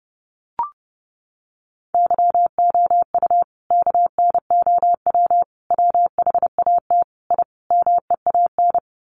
Morse code, heard as a keyed tone, about 30 wpm, 700 Hz. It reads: YOU KNOW WHAT I MEAN
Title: Sets of 5 Words 30wpm